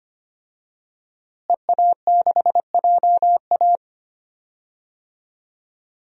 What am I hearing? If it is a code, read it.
Morse code, heard as a keyed tone, about 25 wpm, 700 Hz. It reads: EA6JA
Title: DXCC-100 Call Signs - 5 Characters - Repeated 2x - 25wpm